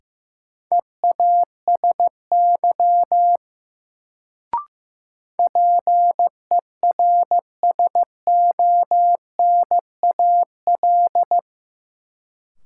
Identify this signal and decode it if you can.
Morse code, heard as a keyed tone, about 16 wpm, 700 Hz. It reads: EASY PERSONAL
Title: Sentences from Top 700 Words - New Word Drill - 15wpm